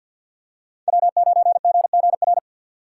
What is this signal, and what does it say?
Morse code, heard as a keyed tone, about 52 wpm, 700 Hz. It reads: W9GGR
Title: USA Call Signs - 5 Characters - 6x Character Spacing - 50wpm